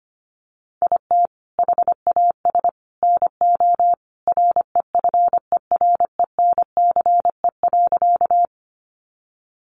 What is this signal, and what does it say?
Morse code, heard as a keyed tone, about 25 wpm, 700 Hz. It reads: IT HAS NO REFERENCE.
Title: Sentences from Top 1200 Words 25wpm